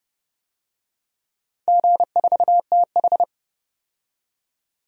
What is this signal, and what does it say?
Morse code, heard as a keyed tone, about 30 wpm, 700 Hz. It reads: G4TH
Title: DXCC-100 Call Signs - 4 Characters - 30wpm